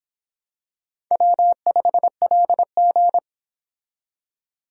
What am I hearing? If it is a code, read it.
Morse code, heard as a keyed tone, about 23 wpm, 700 Hz. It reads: W5LG